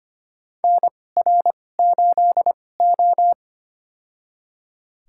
Morse code, heard as a keyed tone, about 25 wpm, spaced as for 18 wpm, 700 Hz. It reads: NR8O
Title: USA Call Signs - 4 Characters - 2x Character Spacing - 25wpm